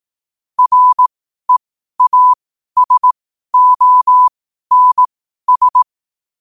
Morse code, heard as a keyed tone, about 18 wpm, spaced as for 12 wpm, 1000 Hz. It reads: REASONS
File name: qotdfiles / 12wpm